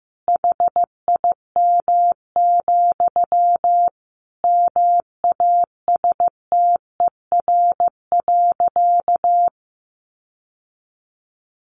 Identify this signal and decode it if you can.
Morse code, heard as a keyed tone, about 14 wpm, 700 Hz. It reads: HIM, MASTER.
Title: Sentences from Top 1300 Words 15wpm